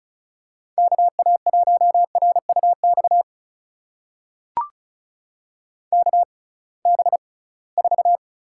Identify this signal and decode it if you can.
Morse code, heard as a keyed tone, about 36 wpm, spaced as for 23 wpm, 700 Hz. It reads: KA1RUX KB4
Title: USA Call Signs - 6 Characters - 6x Character Spacing - 35wpm